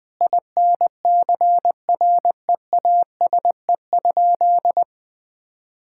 Morse code, heard as a keyed tone, about 20 wpm, 700 Hz. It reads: INCREASE?
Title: Sentences from Top 1000 Words - Review Part 5 - 20wpm